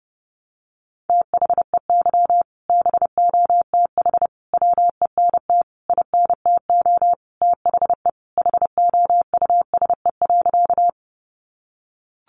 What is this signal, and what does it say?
Morse code, heard as a keyed tone, about 30 wpm, 700 Hz. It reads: THEY BOTH WENT INTO THE HOUSE.
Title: Sentences from Top 600 Words 30wpm